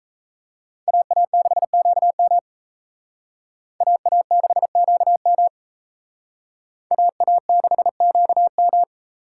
Morse code, heard as a keyed tone, about 34 wpm, 700 Hz. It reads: AA6QM AA6QM AA6QM
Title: Speed Racing - USA 2x2 Call Signs 28wpm